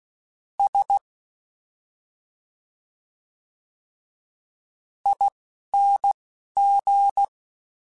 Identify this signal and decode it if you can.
Morse code, heard as a keyed tone, about 16 wpm, 800 Hz. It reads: S I N G
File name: Special INGs-mp3s / Special INGs-mp3s / Slow ing @10